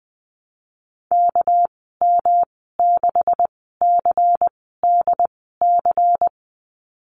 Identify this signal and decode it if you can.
Morse code, heard as a keyed tone, about 20 wpm, spaced as for 15 wpm, 700 Hz. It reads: KM6CDC
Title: 200 Lesson 32g - Letter k - Call Signs 20wpm